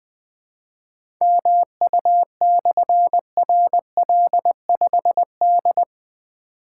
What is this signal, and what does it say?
Morse code, heard as a keyed tone, about 20 wpm, 700 Hz. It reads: MU/RL5D
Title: Difficult Call Signs 20wpm